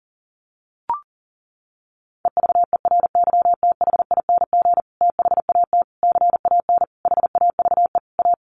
Morse code, heard as a keyed tone, about 37 wpm, 700 Hz. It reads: EVERYTHING THAT CAN HAVE A
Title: Sentences from Top 1300 Words 40wpm